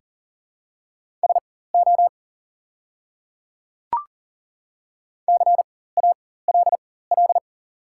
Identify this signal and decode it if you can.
Morse code, heard as a keyed tone, about 40 wpm, spaced as for 19 wpm, 700 Hz. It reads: SO CALL
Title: 4x Character Spacing - Top 200 Words - 40wpm